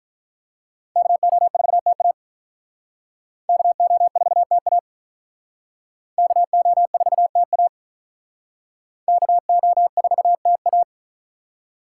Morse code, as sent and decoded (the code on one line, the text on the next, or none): -.- --- ....- - .- / -.- --- ....- - .- / -.- --- ....- - .- / -.- --- ....- - .-
KO4TA KO4TA KO4TA KO4TA